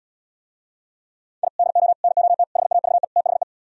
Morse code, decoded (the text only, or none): I DO NOT HAVE THE